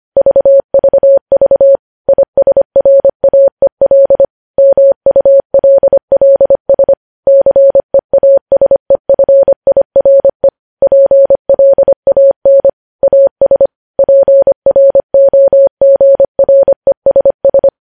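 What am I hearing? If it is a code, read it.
Morse code, heard as a keyed tone, about 25 wpm, 550 Hz. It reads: VVV ISRAEL MULLS CEASEFIRE PLAN AS PROGRESS
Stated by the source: Morse code news headlines sent at 25 wpm.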